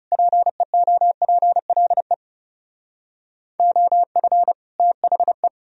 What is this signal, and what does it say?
Morse code, heard as a keyed tone, about 38 wpm, 700 Hz. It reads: PEOPLE OF THE